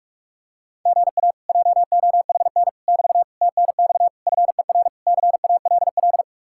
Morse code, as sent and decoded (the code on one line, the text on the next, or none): --. .- / .--- --- .... -. / -...- / - -. -..- / ..-. .-. / -.-. .- .-.. .-..
GA JOHN <BT> TNX FER CALL